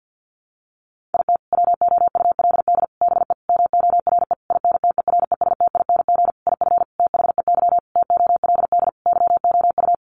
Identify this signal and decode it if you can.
Morse code, heard as a keyed tone, about 50 wpm, 700 Hz. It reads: IT WOULD BE MORE INTERESTING IF THEY TOLD YOU
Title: Sentences from Top 1000 Words 50wpm